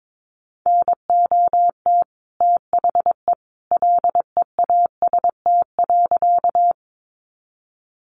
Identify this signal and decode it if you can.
Morse code, heard as a keyed tone, about 22 wpm, 700 Hz. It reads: NOT THE LEAST.
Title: Sentences from Top 700 Words 22wpm